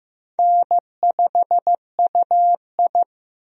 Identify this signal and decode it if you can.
Morse code, heard as a keyed tone, about 15 wpm, 700 Hz. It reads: N5UI
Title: USA Call Signs - 4 Characters - Repeated 3x - 15wpm